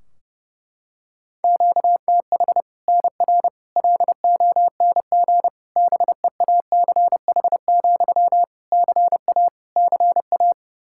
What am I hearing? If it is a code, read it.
Morse code, heard as a keyed tone, about 30 wpm, 700 Hz. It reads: QTH NR LONG BEACH, CA CA
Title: K1USN SST QSO Practice 30wpm